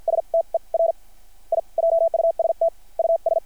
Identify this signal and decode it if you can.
Morse code, heard as a keyed tone, about 50 wpm, spaced as for 41 wpm, 660 Hz. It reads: STER I JUST US
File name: archived / CRSnet